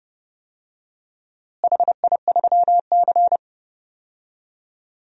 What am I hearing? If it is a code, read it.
Morse code, heard as a keyed tone, about 30 wpm, 700 Hz. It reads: HI3C
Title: DXCC-100 Call Signs - 4 Characters - Repeated 2x - 30wpm